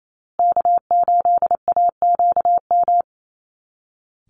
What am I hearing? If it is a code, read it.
Morse code, heard as a keyed tone, about 28 wpm, 700 Hz. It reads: K8AQM
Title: Call Signs Repeated 3x - 28wpm